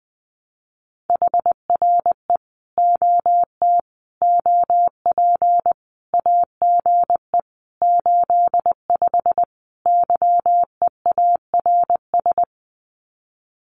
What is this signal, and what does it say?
Morse code, heard as a keyed tone, about 19 wpm, 700 Hz. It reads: HRE OT OP AGE 85 YEARS
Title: Common QSO Phrases 20wpm